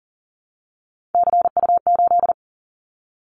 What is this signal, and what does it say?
Morse code, heard as a keyed tone, about 39 wpm, 700 Hz. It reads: CU8
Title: Callsign Prefixes for Top 50 DXCC - 40wpm